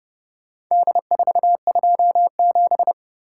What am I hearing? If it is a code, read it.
Morse code, heard as a keyed tone, about 29 wpm, 700 Hz. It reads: D427